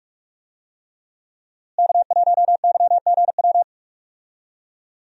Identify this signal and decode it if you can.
Morse code, heard as a keyed tone, about 46 wpm, 700 Hz. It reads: K1YGW
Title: USA Call Signs - 5 Characters - Repeated 3x - 45wpm